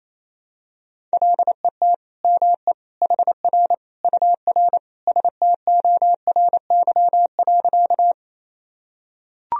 Morse code, heard as a keyed tone, about 28 wpm, 700 Hz. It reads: LET ME HR UR STORY.
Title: Sentences from Top 300 Words and Abbreviations 28wpm